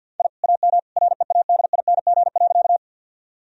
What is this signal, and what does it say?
Morse code, heard as a keyed tone, about 50 wpm, 700 Hz. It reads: I AM READING.
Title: Sentences from Top 1000 Words - Review Part 3 - 50wpm